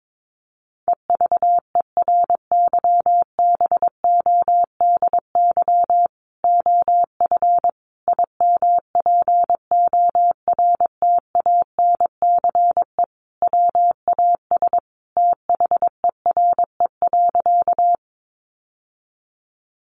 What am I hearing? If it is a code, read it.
Morse code, heard as a keyed tone, about 22 wpm, 700 Hz. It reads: EVERYBODY OF IMPORTANCE WAS THERE.
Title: Sentences from Top 1400 Words 22wpm